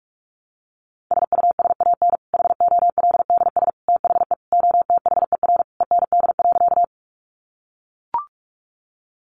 Morse code, heard as a keyed tone, about 44 wpm, 700 Hz. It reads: SUSAN HOLDS THE OTHER END.
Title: Easy Phrases 45wpm